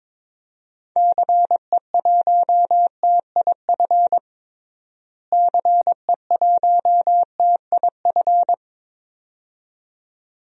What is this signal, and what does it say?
Morse code, heard as a keyed tone, about 22 wpm, 700 Hz. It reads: CE1TIF CE1TIF
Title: DXCC-100 Call Signs - 6 Characters - Repeated 2x - 22wpm